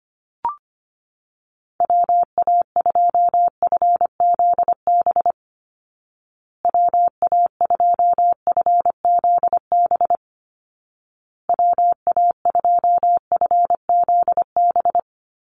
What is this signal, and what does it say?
Morse code, heard as a keyed tone, about 25 wpm, 700 Hz. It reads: WA2FZB WA2FZB WA2FZB
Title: Call Signs Repeated 2x - 25wpm